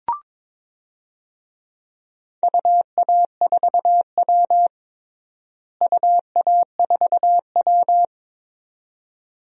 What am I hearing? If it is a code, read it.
Morse code, heard as a keyed tone, about 22 wpm, 700 Hz. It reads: UA4W UA4W
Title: DXCC-100 Call Signs - 4 Characters - Repeated 2x - 22wpm